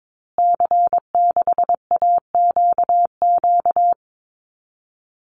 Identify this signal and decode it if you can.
Morse code, heard as a keyed tone, about 22 wpm, 700 Hz. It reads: C6AQQ
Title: Call Signs Repeated 2x - 22wpm